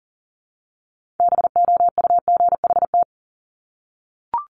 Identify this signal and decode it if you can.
Morse code, heard as a keyed tone, about 42 wpm, 700 Hz. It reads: BOUGHT
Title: Multi-letter Phonemes Repeated 3x 40wpm